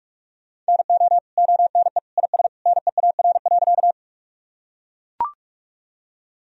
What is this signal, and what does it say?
Morse code, heard as a keyed tone, about 43 wpm, 700 Hz. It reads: NO ONE IS NEAR.
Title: Sentences from Top 1000 Words - Review Part 1 - 45wpm